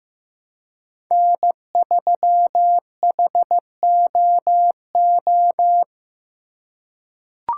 Morse code, heard as a keyed tone, about 15 wpm, 700 Hz. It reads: N3HOO
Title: USA Call Signs - 5 Characters - 3x Character Spacing - 15wpm